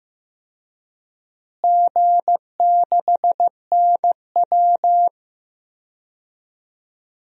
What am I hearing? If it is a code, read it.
Morse code, heard as a keyed tone, about 15 wpm, 700 Hz. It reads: G6NW
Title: DXCC-100 Call Signs - 4 Characters - Repeated 2x - 15wpm